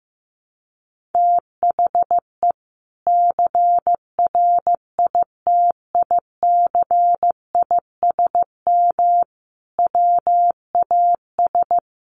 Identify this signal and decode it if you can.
Morse code, heard as a keyed tone, about 15 wpm, 700 Hz. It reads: THE CRITICISM WAS
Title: Sentences from Top 1300 Words 15wpm